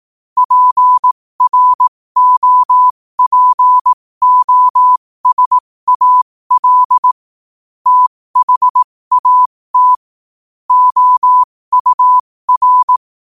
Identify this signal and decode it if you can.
Morse code, heard as a keyed tone, about 18 wpm, spaced as for 15 wpm, 1000 Hz. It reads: PROPOSAL THAT OUR
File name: qotdfiles / 15wpm